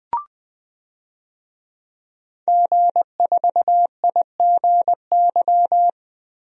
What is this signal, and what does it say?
Morse code, heard as a keyed tone, about 20 wpm, 700 Hz. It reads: G4IGY
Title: DXCC-100 Call Signs - 5 Characters - Repeated 2x - 20wpm